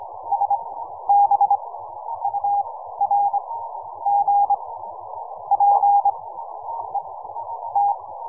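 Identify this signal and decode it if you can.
Morse code, heard as a keyed tone, about 24 wpm, spaced as for 11 wpm, 810 Hz. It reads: SBURG PET
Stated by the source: QRP Morse code signal.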